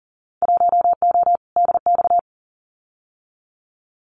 Morse code, heard as a keyed tone, about 42 wpm, 700 Hz. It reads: 1O DX
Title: Winter Field Day Exchanges Encore 40wpm